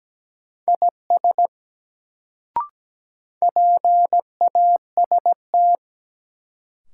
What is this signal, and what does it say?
Morse code, heard as a keyed tone, about 17 wpm, 700 Hz. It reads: IS PAST
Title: Sets of 2 Words - New Word Drill - 17wpm